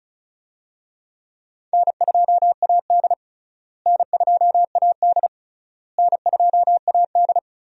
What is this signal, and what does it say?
Morse code, heard as a keyed tone, about 36 wpm, 700 Hz. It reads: N2AD N2AD N2AD
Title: USA Call Signs - 4 Characters - Repeated 3x - 35wpm